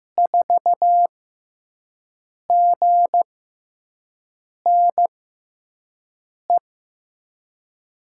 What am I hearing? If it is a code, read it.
Morse code, heard as a keyed tone, about 15 wpm, spaced as for 5 wpm, 700 Hz. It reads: 4GNE
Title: USA Call Signs - 6x Character Spacing - 15wpm